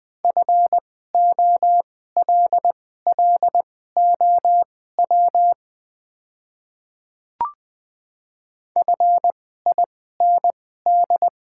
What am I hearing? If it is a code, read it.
Morse code, heard as a keyed tone, about 20 wpm, spaced as for 14 wpm, 700 Hz. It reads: FOLLOW FIND
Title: 2x Character Spacing - Top 200 Words - 20wpm